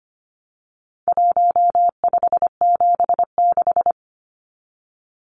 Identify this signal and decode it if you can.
Morse code, heard as a keyed tone, about 25 wpm, 700 Hz. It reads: 1576
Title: Numbers - Contest Style - 4-Digit 25wpm